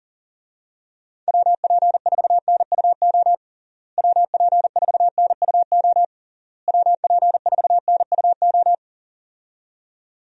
Morse code, heard as a keyed tone, about 40 wpm, 700 Hz. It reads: WP4NUO WP4NUO WP4NUO
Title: USA Call Signs - 6 Characters - Repeated 3x - 40wpm